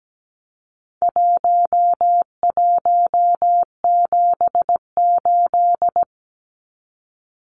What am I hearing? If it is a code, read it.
Morse code, heard as a keyed tone, about 17 wpm, 700 Hz. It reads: 1178
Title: Numbers - 4-Digit 17wpm